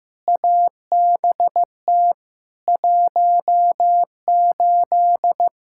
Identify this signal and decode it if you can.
Morse code, heard as a keyed tone, about 16 wpm, 700 Hz. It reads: ABT 18
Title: Practice QSO - Exchange 3 - More Info - 15wpm